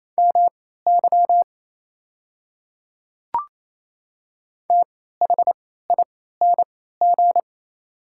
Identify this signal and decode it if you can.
Morse code, heard as a keyed tone, about 27 wpm, spaced as for 16 wpm, 690 Hz. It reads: MY THING
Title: 3x Character Spacing - Top 100 Words - 28wpm